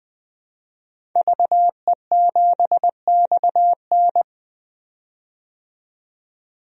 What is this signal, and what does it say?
Morse code, heard as a keyed tone, about 20 wpm, 700 Hz. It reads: VE7XN
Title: DXCC-100 Call Signs - 5 Characters - 20wpm